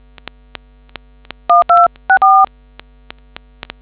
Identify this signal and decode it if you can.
DTMF, heard as telephone touch tones, tones of about 170 ms. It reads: 1264